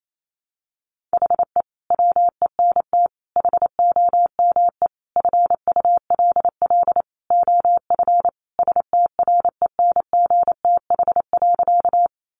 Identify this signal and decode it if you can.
Morse code, heard as a keyed tone, about 28 wpm, 700 Hz. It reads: HE WENT HOME FULL OF STRENGTH.
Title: Sentences from Top 1300 Words 28wpm